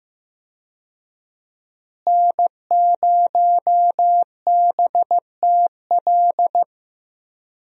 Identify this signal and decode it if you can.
Morse code, heard as a keyed tone, about 15 wpm, 700 Hz. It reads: N0BTL
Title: USA Call Signs - 5 Characters - Repeated 3x - 15wpm